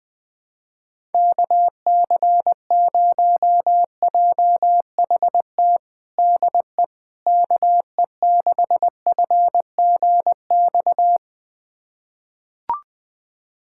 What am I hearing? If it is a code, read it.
Morse code, heard as a keyed tone, about 21 wpm, 700 Hz. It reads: KC0JHT DE KE6FGX
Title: USA Callsign DE Callsign - 6 Characters - 20wpm